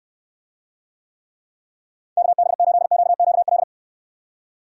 Morse code, heard as a keyed tone, about 62 wpm, 700 Hz. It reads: KB8QQL